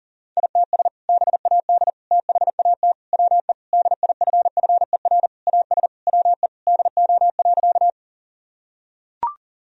Morse code, heard as a keyed tone, about 40 wpm, 700 Hz. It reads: ITS BAD THAT WE DIFFER AS WE DO.
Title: Sentences from Top 1000 Words - Review Part 1 - 40wpm